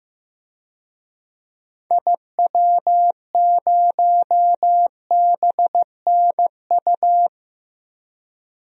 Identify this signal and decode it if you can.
Morse code, heard as a keyed tone, about 15 wpm, 700 Hz. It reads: IW0BNU
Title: DXCC-100 Call Signs - 6 Characters - 15wpm